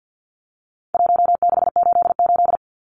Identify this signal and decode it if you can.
Morse code, heard as a keyed tone, about 50 wpm, 700 Hz. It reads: 1688
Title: Numbers - Contest Style - 4-Digit 50wpm